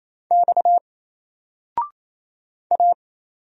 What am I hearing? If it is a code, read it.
Morse code, heard as a keyed tone, about 27 wpm, 700 Hz. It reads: X A
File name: Single-Letter-Number-28wpm